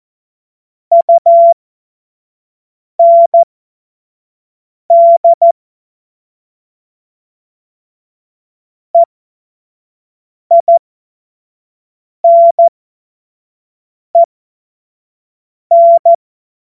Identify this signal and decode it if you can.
Morse code, heard as a keyed tone, about 14 wpm, spaced as for 5 wpm, 680 Hz. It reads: UND EINEN
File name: Morsekurs / Witze / Witz 23 Januar 2024 25 F70